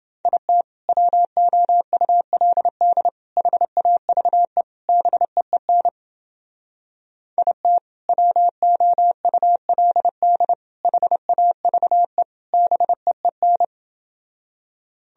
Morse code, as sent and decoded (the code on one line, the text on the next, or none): .. - / .-- --- ..- .-.. -.. / .... .- ...- . / -... . -. / .. - / .-- --- ..- .-.. -.. / .... .- ...- . / -... . -.
IT WOULD HAVE BEEN IT WOULD HAVE BEEN